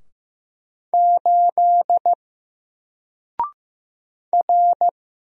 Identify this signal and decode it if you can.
Morse code, heard as a keyed tone, about 15 wpm, 700 Hz. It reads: 8 R
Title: Single Letter-Number Alt 15wpm